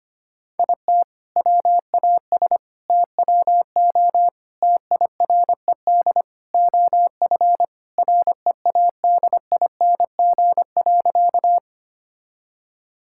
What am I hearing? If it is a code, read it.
Morse code, heard as a keyed tone, about 25 wpm, 700 Hz. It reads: IT WAS TWO TIRED OF READING.
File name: practice / 25wpm / All-Age Appropriate Jokes - 25wpm